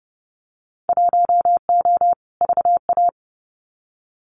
Morse code, heard as a keyed tone, about 30 wpm, 700 Hz. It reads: 1O VA
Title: Winter Field Day Exchanges 30wpm